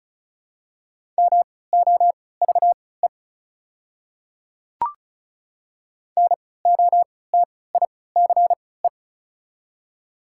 Morse code, heard as a keyed tone, about 34 wpm, spaced as for 20 wpm, 690 Hz. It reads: MOVE NOTICE
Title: 3x Character Spacing - Top 500 Words - 35wpm